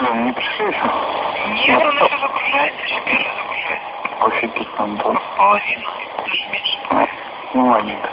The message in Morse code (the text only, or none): none